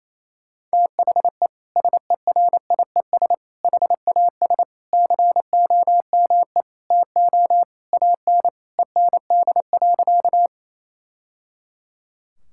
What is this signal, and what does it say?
Morse code, heard as a keyed tone, about 28 wpm, 700 Hz. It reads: THE SERIES HAS COME TO AN END.
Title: Sentences from Top 1000 Words - Review Part 5 - 28wpm